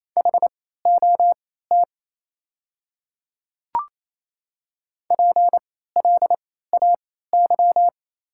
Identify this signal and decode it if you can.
Morse code, heard as a keyed tone, about 28 wpm, spaced as for 16 wpm, 700 Hz. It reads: HOT PLAY
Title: 3x Character Spacing - Top 200 Words - 28wpm